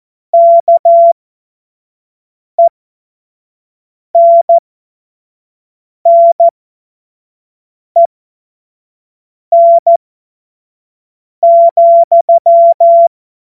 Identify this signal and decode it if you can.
Morse code, heard as a keyed tone, about 14 wpm, spaced as for 5 wpm, 680 Hz. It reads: KENNEN,
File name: Morsekurs / Witze / Witz 18. Januar 2022, 25 F70